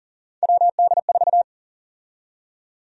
Morse code, heard as a keyed tone, about 40 wpm, 700 Hz. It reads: WD4
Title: USA Call Sign Prefixes - 40wpm